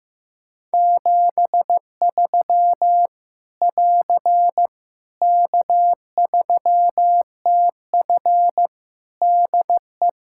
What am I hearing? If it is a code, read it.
Morse code, heard as a keyed tone, about 16 wpm, 700 Hz. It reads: 73 <AR> K3TF DE
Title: K1USN SST QSO Practice 15wpm